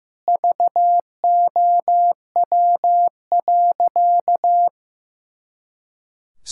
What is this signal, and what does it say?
Morse code, heard as a keyed tone, about 15 wpm, 700 Hz. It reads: VOW.
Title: Pangram Challenge v2 - 15wpm